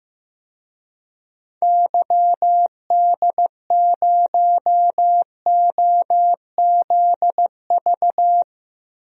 Morse code, heard as a keyed tone, about 15 wpm, 700 Hz. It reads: YD0OZV
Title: DXCC-100 Call Signs - 6 Characters - Repeated 2x - 15wpm